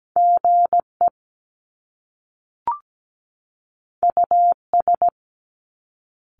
Morse code, heard as a keyed tone, about 17 wpm, 700 Hz. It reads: GE US
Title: Top Two Letter Combinations in Words 17wpm